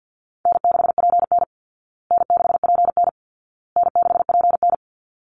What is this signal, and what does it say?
Morse code, heard as a keyed tone, about 49 wpm, 700 Hz. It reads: N6PN N6PN N6PN
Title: Easy Call Signs Repeated 3x - 50wpm